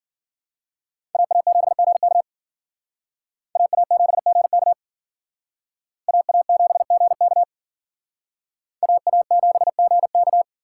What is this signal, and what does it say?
Morse code, heard as a keyed tone, about 49 wpm, 700 Hz. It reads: AA7GK AA7GK AA7GK AA7GK
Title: Speed Racing - USA 2x2 Call Signs 40wpm